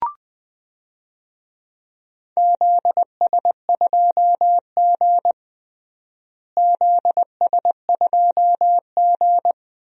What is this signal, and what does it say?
Morse code, heard as a keyed tone, about 20 wpm, 700 Hz. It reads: ZS2G ZS2G
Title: DXCC-100 Call Signs - 4 Characters - Repeated 2x - 20wpm